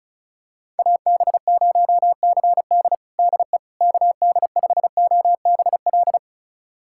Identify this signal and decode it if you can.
Morse code, heard as a keyed tone, about 34 wpm, 700 Hz. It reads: AB0CD DE KD5OBL
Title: USA Callsign DE Callsign 35wpm